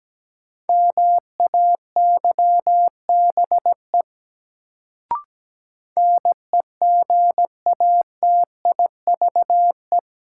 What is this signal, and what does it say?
Morse code, heard as a keyed tone, about 17 wpm, 700 Hz. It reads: MAYBE NEGATIVE
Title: Sentences from Top 1000 Words - New Word Drill - 17wpm